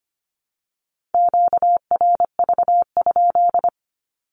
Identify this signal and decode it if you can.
Morse code, heard as a keyed tone, about 25 wpm, 700 Hz. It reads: QRV?
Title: Q-Codes 25wpm